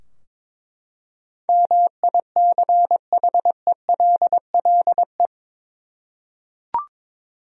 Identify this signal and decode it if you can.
Morse code, heard as a keyed tone, about 22 wpm, 700 Hz. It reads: MICHELLE
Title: USA Common Names - 22wpm